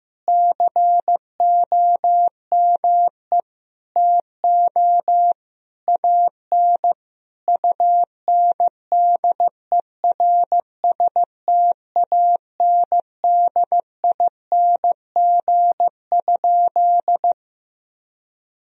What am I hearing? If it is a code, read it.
Morse code, heard as a keyed tone, about 16 wpm, 700 Hz. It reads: COME TO AN UNDERSTANDING?
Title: Sentences from Top 1000 Words - Review Part 3 - 15wpm